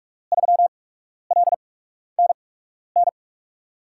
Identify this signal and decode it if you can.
Morse code, heard as a keyed tone, about 44 wpm, spaced as for 12 wpm, 700 Hz. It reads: 3LNN
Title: USA Call Signs - 6 Characters - 8x Character Spacing - 45wpm